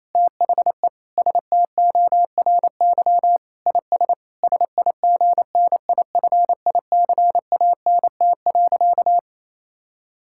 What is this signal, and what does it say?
Morse code, heard as a keyed tone, about 28 wpm, 700 Hz. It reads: THE STORY IS SIGNIFICANT.
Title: Sentences from Top 1000 Words - Review Part 6 - 28wpm